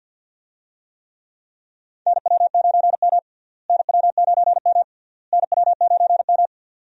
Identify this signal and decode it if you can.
Morse code, heard as a keyed tone, about 50 wpm, 700 Hz. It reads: NW9M NW9M NW9M
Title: USA Call Signs - 4 Characters - Repeated 3x - 50wpm